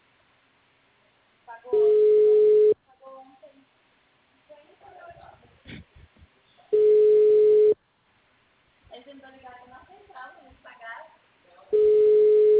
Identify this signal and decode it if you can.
Ringback tone (Europe, 425 Hz), with speech.